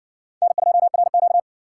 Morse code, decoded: N2RQ

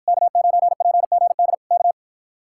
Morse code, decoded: K9PGD K

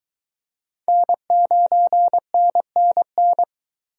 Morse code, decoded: N9NNN

